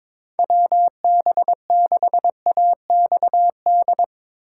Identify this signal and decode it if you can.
Morse code, heard as a keyed tone, about 24 wpm, 700 Hz. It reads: WB6AXD
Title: Speed Racing - USA Call Signs 22wpm